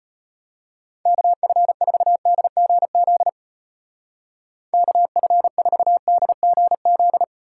Morse code, heard as a keyed tone, about 31 wpm, 700 Hz. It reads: KF4DGZ KF4DGZ
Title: Speed Racing - USA 2x3 Call Signs 25wpm